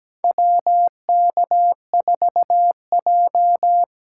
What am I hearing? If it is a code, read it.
Morse code, heard as a keyed tone, about 17 wpm, 700 Hz. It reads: WK4J
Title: USA Call Signs - 4 Characters - 3x Character Spacing - 17wpm